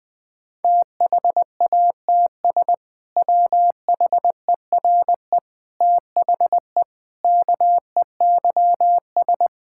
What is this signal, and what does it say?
Morse code, heard as a keyed tone, about 21 wpm, 700 Hz. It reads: THATS WHERE THE KEYS
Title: Sentences with 2-Word Phrases - 20wpm